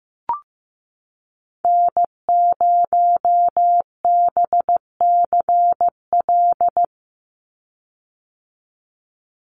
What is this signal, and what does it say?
Morse code, heard as a keyed tone, about 15 wpm, 700 Hz. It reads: N0BCL
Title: USA Call Signs 15wpm